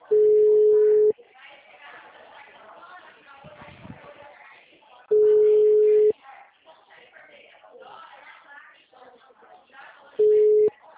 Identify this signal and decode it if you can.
Ringback tone (Europe, 425 Hz), with speech.